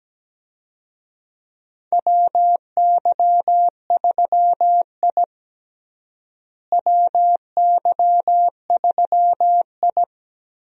Morse code, heard as a keyed tone, about 17 wpm, 700 Hz. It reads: WY3I WY3I
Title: USA Call Signs - 4 Characters - Repeated 3x - 17wpm